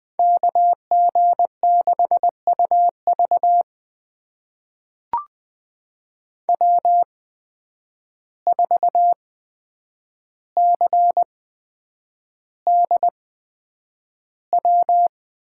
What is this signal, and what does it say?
Morse code, heard as a keyed tone, about 20 wpm, spaced as for 9 wpm, 700 Hz. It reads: KG6UV W4CDW
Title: USA Call Signs - 5 Characters - 8x Character Spacing - 20wpm